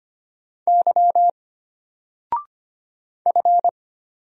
Morse code, heard as a keyed tone, about 25 wpm, 700 Hz. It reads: Y F
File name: Single-Letter-Number-25wpm